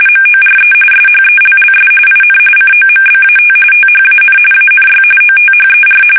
RTTY, about 45.45 baud. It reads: MORE SHELLS IN COMBAT THAN ANY OTHER
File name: fsk850 / battleshipnj